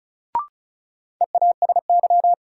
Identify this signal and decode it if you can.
Morse code, heard as a keyed tone, about 36 wpm, 700 Hz. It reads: EASY
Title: Sentences from Top 700 Words - New Word Drill - 35wpm